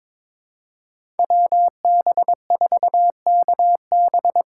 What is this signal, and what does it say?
Morse code, heard as a keyed tone, about 22 wpm, 700 Hz. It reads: WB4KB
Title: USA Call Signs - 6 Characters - 6x Character Spacing - 22wpm